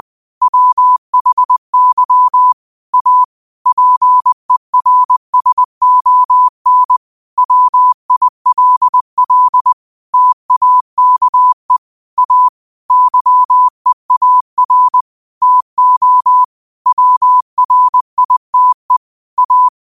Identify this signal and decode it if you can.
Morse code, heard as a keyed tone, about 20 wpm, 1000 Hz. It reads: WHY A PERSON WILL TAKE A YEAR TO WRITE A
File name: qotdfiles / 20wpm